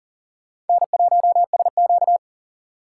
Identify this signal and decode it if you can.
Morse code, heard as a keyed tone, about 51 wpm, 700 Hz. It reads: N1SQ